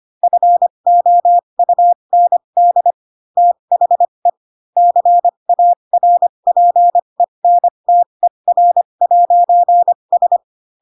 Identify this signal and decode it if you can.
Morse code, heard as a keyed tone, about 25 wpm, spaced as for 22 wpm, 700 Hz. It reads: FOUND THE CARPENTER'S
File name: Pinocchio / Pinocchio ch2 18 25 ik1hge